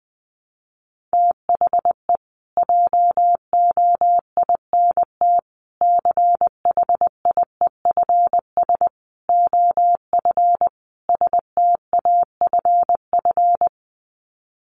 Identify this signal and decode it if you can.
Morse code, heard as a keyed tone, about 20 wpm, 700 Hz. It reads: THE JOINT CHIEFS OF STAFF
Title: Sets of 5 Words Encore2 20wpm